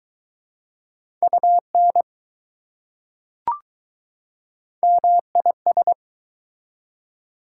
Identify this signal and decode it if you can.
Morse code, heard as a keyed tone, about 20 wpm, 700 Hz. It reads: UN MIS